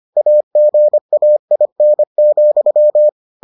Morse code, heard as a keyed tone, about 25 wpm, 600 Hz. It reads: AGAIN,